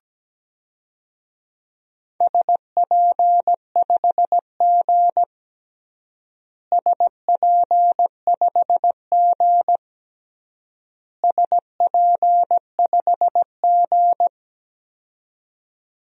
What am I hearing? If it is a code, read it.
Morse code, heard as a keyed tone, about 17 wpm, 700 Hz. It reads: SP5G SP5G SP5G
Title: DXCC-100 Call Signs - 4 Characters - Repeated 3x - 17wpm